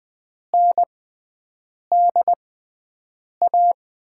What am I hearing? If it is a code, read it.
Morse code, heard as a keyed tone, about 20 wpm, spaced as for 7 wpm, 700 Hz. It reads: NDA
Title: USA Call Signs - 6 Characters - 6x Character Spacing - 20wpm